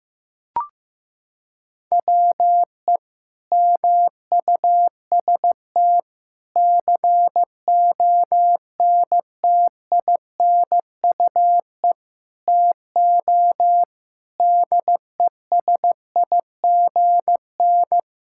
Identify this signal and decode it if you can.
Morse code, heard as a keyed tone, about 15 wpm, 700 Hz. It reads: WE MUST CONTINUE TO DESIGN
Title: Sentences from Top 1000 Words - Review Part 6 - 15wpm